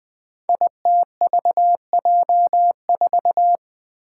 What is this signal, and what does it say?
Morse code, heard as a keyed tone, about 20 wpm, 700 Hz. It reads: ITVJ4
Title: Five Challenging Letter-Number Combinations 20wpm